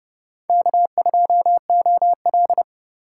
Morse code, heard as a keyed tone, about 29 wpm, 700 Hz. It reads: K2OL